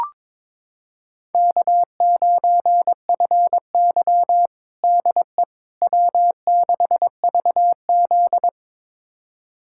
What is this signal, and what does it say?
Morse code, heard as a keyed tone, about 22 wpm, 700 Hz. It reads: K9FY DE W6VZ
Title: USA Callsign DE Callsign - 4 Characters - 22wpm